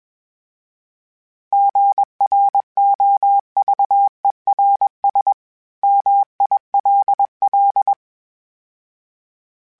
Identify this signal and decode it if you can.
Morse code, heard as a keyed tone, about 21 wpm, 800 Hz. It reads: GROVERS MILL
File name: fromSMs / FromSMs mp3s / Cities@20